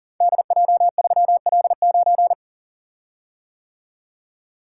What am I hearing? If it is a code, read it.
Morse code, heard as a keyed tone, about 40 wpm, 700 Hz. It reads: DJ3L9
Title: Code Groups - 5 Letter:Numbers - 40wpm